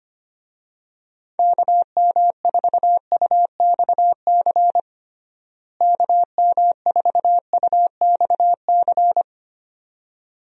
Morse code, heard as a keyed tone, about 25 wpm, 700 Hz. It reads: KM4UXC KM4UXC
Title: USA Call Signs - 6 Characters - Repeated 2x - 25wpm